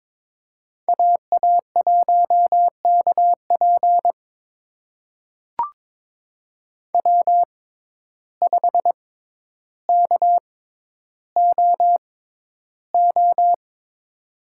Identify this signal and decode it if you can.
Morse code, heard as a keyed tone, about 22 wpm, spaced as for 11 wpm, 700 Hz. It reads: AA1KP W5KOO
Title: USA Call Signs - 6x Character Spacing - 22wpm